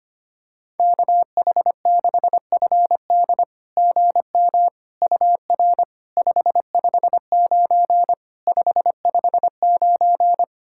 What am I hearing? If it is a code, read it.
Morse code, heard as a keyed tone, about 25 wpm, 700 Hz. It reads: KH6FD GM UR 559 559
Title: SOTA QSO Practice 25wpm